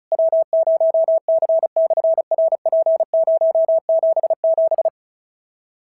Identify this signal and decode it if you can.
Morse code, heard as a keyed tone, about 36 wpm, 650 Hz. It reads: W0C/RP077
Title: SOTA QSO Practice 35wpm